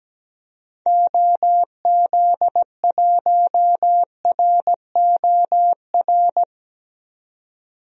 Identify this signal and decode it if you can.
Morse code, heard as a keyed tone, about 17 wpm, 700 Hz. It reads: OZ1ROR
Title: DXCC-100 Call Signs - 6 Characters - 17wpm